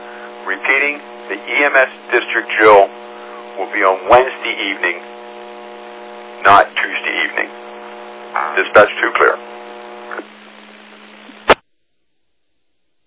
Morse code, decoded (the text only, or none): none